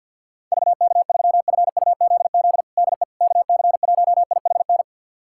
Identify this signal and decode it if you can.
Morse code, heard as a keyed tone, about 51 wpm, 700 Hz. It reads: VK3FUZZ DE KC1IHN
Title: USA Callsign DE Callsign 50wpm